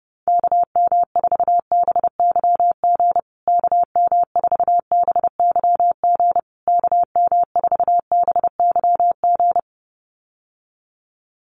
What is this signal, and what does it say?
Morse code, heard as a keyed tone, about 30 wpm, 700 Hz. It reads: KM4BYG KM4BYG KM4BYG
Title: USA Call Signs Repeated 3x 30wpm